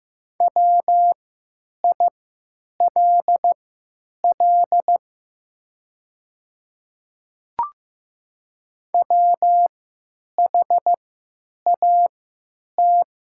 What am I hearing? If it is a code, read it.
Morse code, heard as a keyed tone, about 15 wpm, spaced as for 9 wpm, 700 Hz. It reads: WILL WHAT
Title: 3x Character Spacing - Top 100 Words - 15wpm